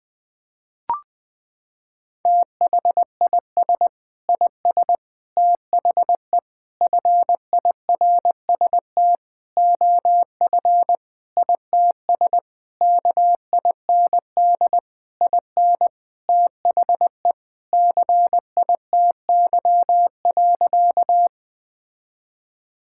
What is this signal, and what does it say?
Morse code, heard as a keyed tone, about 21 wpm, 700 Hz. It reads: THIS IS THE FIRST OF ITS KIND IN THE CITY.
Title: Sentences with 6-Word Phrases - 20wpm